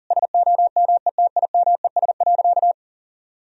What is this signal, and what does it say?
Morse code, heard as a keyed tone, about 40 wpm, 700 Hz. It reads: SOMETIMES.
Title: Sentences from Top 1000 Words - Review Part 3 - 40wpm